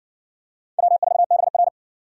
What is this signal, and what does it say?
Morse code, heard as a keyed tone, about 57 wpm, 700 Hz. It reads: W4BR